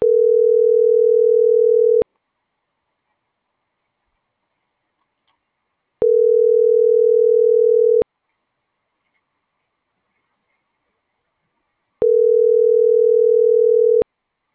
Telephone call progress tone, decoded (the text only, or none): ringback tone